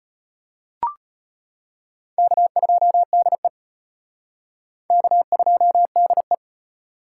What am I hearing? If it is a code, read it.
Morse code, heard as a keyed tone, about 33 wpm, 700 Hz. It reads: K2DE K2DE